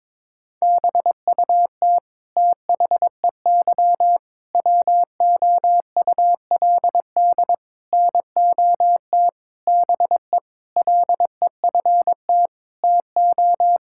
Morse code, heard as a keyed tone, about 22 wpm, 700 Hz. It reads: BUT THEY WOULD NOT BE LEFT TO
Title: Sentences from Top 1000 Words - Review Part 5 - 22wpm